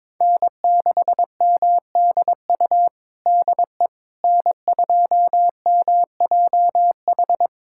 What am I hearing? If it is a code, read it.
Morse code, heard as a keyed tone, about 22 wpm, 700 Hz. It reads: N6MDU DE N2MJH
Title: USA Callsign DE Callsign - 5 Characters - 22wpm